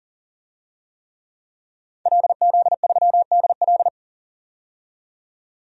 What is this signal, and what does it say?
Morse code, heard as a keyed tone, about 40 wpm, 700 Hz. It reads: LZ3DL
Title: DXCC-100 Call Signs - 5 Characters - 40wpm